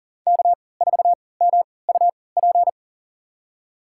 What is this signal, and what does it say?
Morse code, heard as a keyed tone, about 40 wpm, spaced as for 23 wpm, 700 Hz. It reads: K4MUP
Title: USA Call Signs - 5 Characters - 3x Character Spacing - 40wpm